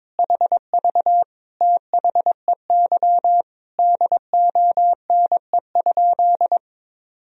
Morse code, heard as a keyed tone, about 22 wpm, 700 Hz. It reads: HV THEY DONE?
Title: Sentences from Top 500 Words and Abbreviations 22wpm